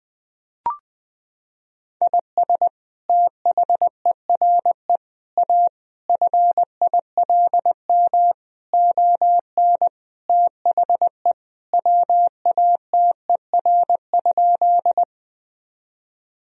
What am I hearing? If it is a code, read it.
Morse code, heard as a keyed tone, about 21 wpm, 700 Hz. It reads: IS THERE A FILM ON THE WATER?
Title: Sentences from Top 1000 Words - Review Part 3 - 20wpm